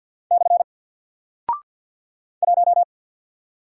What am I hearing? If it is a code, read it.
Morse code, heard as a keyed tone, about 50 wpm, 700 Hz. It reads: / 1
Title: Single Character (Letter:Number:Punctuation:Prosign) 50wpm